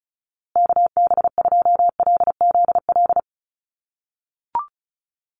Morse code, heard as a keyed tone, about 34 wpm, 700 Hz. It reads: KB2LZL
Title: USA Call Signs Repeated 3x 35wpm